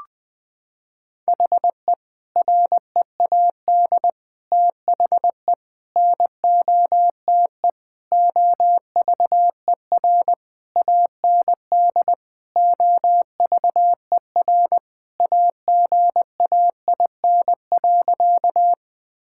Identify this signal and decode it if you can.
Morse code, heard as a keyed tone, about 20 wpm, 700 Hz. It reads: HE READ THE NOTE OVER AND OVER AGAIN.
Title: Sentences with 4-Word Phrases - 20wpm